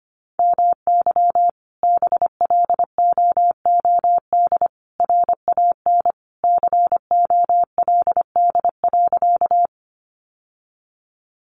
Morse code, heard as a keyed tone, about 25 wpm, 700 Hz. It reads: MY BLOOD RAN COLD.
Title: Sentences from Top 1100 Words 25wpm